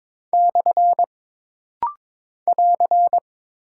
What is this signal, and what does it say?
Morse code, heard as a keyed tone, about 22 wpm, 700 Hz. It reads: / <AR>
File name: practice / 22wpm / Single Punctuation-Prosign - Rapid-Fire - 22wpm